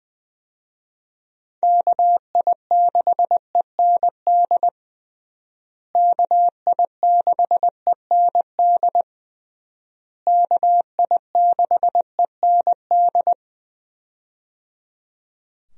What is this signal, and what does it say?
Morse code, heard as a keyed tone, about 21 wpm, 700 Hz. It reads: KI6END KI6END KI6END
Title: USA Call Signs - 6 Characters - Repeated 3x - 20wpm